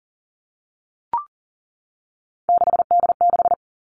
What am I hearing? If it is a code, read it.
Morse code, heard as a keyed tone, about 41 wpm, 700 Hz. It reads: 6D6
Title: Three Challenging Letter-Number Combinations 40wpm